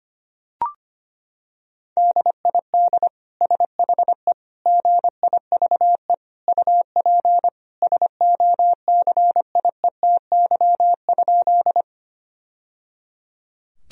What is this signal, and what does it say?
Morse code, heard as a keyed tone, about 25 wpm, 700 Hz. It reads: DID SHE GIVE UP SOCIETY?
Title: Sentences from Top 1000 Words - Review Part 4 - 25wpm